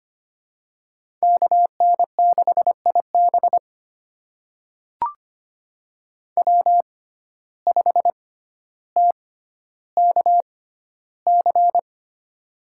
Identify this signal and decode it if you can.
Morse code, heard as a keyed tone, about 25 wpm, spaced as for 13 wpm, 700 Hz. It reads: KN6IB W5TKC
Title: USA Call Signs - 5 Characters - 6x Character Spacing - 25wpm